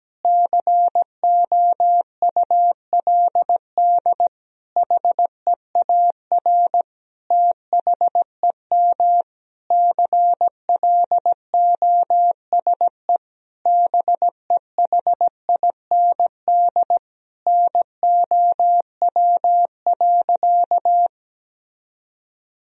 Morse code, heard as a keyed tone, about 17 wpm, 700 Hz. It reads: COULD HEAR THEM CLOSE BEHIND NOW.
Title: Sentences from Top 1000 Words - Review Part 1 - 17wpm